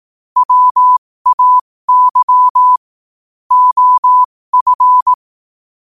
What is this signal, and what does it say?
Morse code, heard as a keyed tone, about 18 wpm, spaced as for 15 wpm, 1000 Hz. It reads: WAY OF
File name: qotdfiles / 15wpm